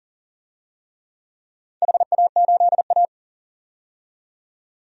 Morse code, heard as a keyed tone, about 40 wpm, 700 Hz. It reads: HA8A
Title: DXCC-100 Call Signs - 4 Characters - 40wpm